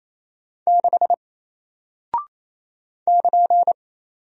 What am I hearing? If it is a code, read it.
Morse code, heard as a keyed tone, about 28 wpm, 700 Hz. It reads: 6 <KN>
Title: Single Character (Letter:Number:Punctuation:Prosign) Alt 28wpm